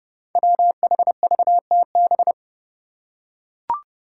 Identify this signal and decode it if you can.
Morse code, heard as a keyed tone, about 29 wpm, 700 Hz. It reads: WHVTB
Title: Five Challenging Letter-Number Combinations 30wpm